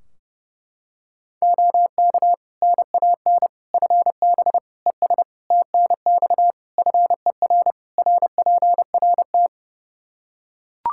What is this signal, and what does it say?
Morse code, heard as a keyed tone, about 32 wpm, 700 Hz. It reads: OK NAN FB ES TNX FER RPRT
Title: Practice QSO - Exchange 2 - Info - 30wpm